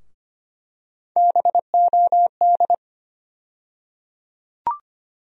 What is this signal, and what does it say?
Morse code, heard as a keyed tone, about 25 wpm, 700 Hz. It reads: BOD
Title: USA Call Sign Postfixes - 3 Letters 25wpm